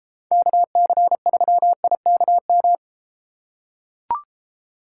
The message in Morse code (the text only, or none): KC3IKM